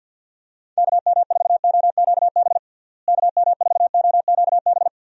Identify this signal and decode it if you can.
Morse code, heard as a keyed tone, about 47 wpm, 700 Hz. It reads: KM4OQB KM4OQB
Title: USA Call Signs - 6 Characters - Repeated 2x - 50wpm